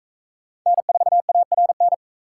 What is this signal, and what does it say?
Morse code, heard as a keyed tone, about 39 wpm, 700 Hz. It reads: N4ARN